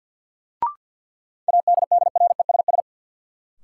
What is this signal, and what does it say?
Morse code, heard as a keyed tone, about 50 wpm, 700 Hz. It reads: ADDRESS
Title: Sets of 5 Words Encore - New Word Drill - 50wpm